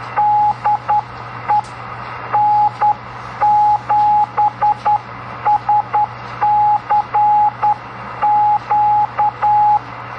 Morse code, heard as a keyed tone, about 10 wpm, 840 Hz, with radio noise. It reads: DE N7SCQ
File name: Key Clicks